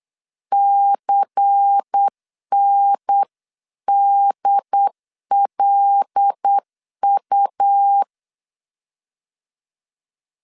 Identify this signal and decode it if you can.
Morse code, heard as a keyed tone, about 8 wpm, 800 Hz. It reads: CNDLU
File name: 8wpm